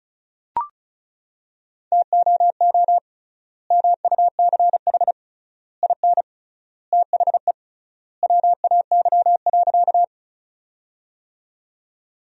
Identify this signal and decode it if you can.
Morse code, heard as a keyed tone, about 34 wpm, 700 Hz. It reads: TOO MUCH IN THE WAY.
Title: 3x Word-Spacing - Sentences from Top 200 Words - 35wpm